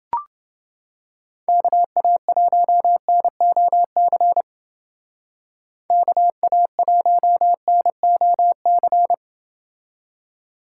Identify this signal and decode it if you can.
Morse code, heard as a keyed tone, about 28 wpm, 700 Hz. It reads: KA1NOC KA1NOC